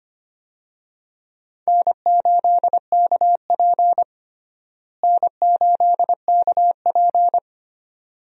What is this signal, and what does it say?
Morse code, heard as a keyed tone, about 25 wpm, 700 Hz. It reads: N8KP N8KP
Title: USA Call Signs - 4 Characters - Repeated 2x - 25wpm